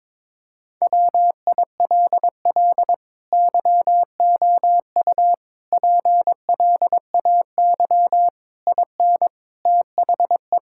Morse code, heard as a keyed tone, about 22 wpm, 700 Hz. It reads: WILL YOU PLAY IN THE
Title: Sentences from Top 1600 Words 22wpm